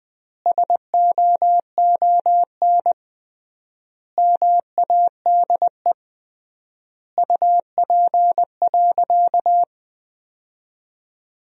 Morse code, heard as a keyed tone, about 20 wpm, 700 Hz. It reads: SOON MADE UP.
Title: 3x Word-Spacing - Sentences from Top 500 Words - 20wpm